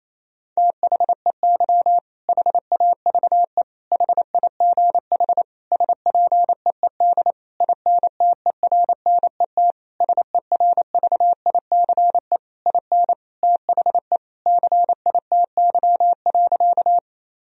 Morse code, heard as a keyed tone, about 28 wpm, 700 Hz. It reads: THEY HAVE HIGH SPEED INTERNET SERVICE IN THE CITY.
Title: Sentences from Top 1000 Words - Review Part 4 - 28wpm